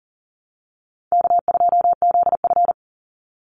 Morse code, heard as a keyed tone, about 40 wpm, 700 Hz. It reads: K2ZF
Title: Call Signs - 40wpm